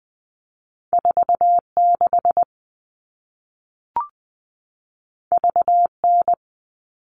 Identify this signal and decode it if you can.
Morse code, heard as a keyed tone, about 20 wpm, 700 Hz. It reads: 46 VN
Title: All Cut Numbers - 2-Digit 20wpm